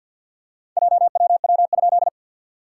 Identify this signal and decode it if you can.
Morse code, heard as a keyed tone, about 52 wpm, 690 Hz. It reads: JWW?
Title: SOTA QSO Practice 50wpm